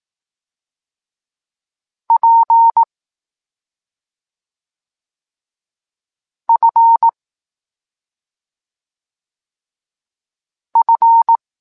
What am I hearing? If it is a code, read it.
Morse code, heard as a keyed tone, about 18 wpm, 920 Hz. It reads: P F F
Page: Tekst 25 – Nauka liter P F